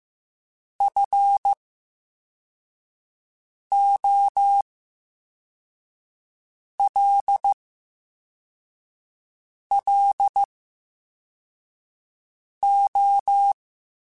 Morse code, heard as a keyed tone, about 15 wpm, spaced as for 4 wpm, 800 Hz. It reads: FOLLO
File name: Special INGs / Special INGs-mp3s / Slow ing @13